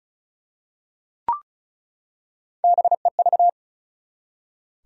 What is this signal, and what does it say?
Morse code, heard as a keyed tone, about 36 wpm, 700 Hz. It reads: BEV
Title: USA Common Names - Max 5 Letters - 35wpm